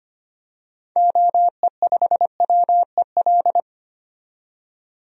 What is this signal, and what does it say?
Morse code, heard as a keyed tone, about 25 wpm, 700 Hz. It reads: OE5WEL
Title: DXCC-100 Call Signs - 6 Characters - 25wpm